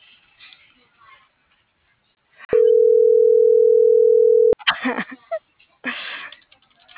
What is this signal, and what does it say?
Ringback tone (North America), 440 + 480 Hz.